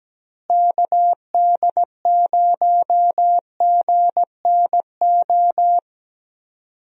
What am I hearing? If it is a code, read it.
Morse code, heard as a keyed tone, about 17 wpm, 700 Hz. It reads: KD0GNO
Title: USA Call Signs - 6 Characters - Repeated 2x - 17wpm